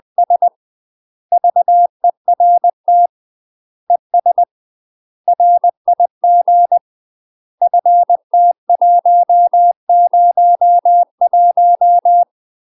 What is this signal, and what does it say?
Morse code, heard as a keyed tone, about 20 wpm, 700 Hz. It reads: S VERT ES RIG FT101
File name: UK MASTER QSOs TRAINING / 20WPM / UK MASTER QSOs 0039